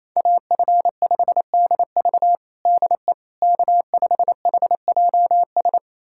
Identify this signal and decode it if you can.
Morse code, heard as a keyed tone, about 28 wpm, 700 Hz. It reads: AF5DV DE K5HJS
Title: USA Callsign DE Callsign - 5 Characters - 28wpm